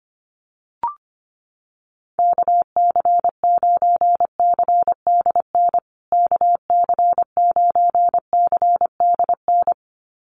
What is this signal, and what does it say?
Morse code, heard as a keyed tone, about 25 wpm, 700 Hz. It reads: KC9CDN KC9CDN
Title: USA Call Signs Repeated 2x 25wpm